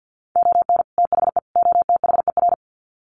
Morse code, heard as a keyed tone, about 48 wpm, 700 Hz. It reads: ON THE OTHER
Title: Sets of 3 Words 50wpm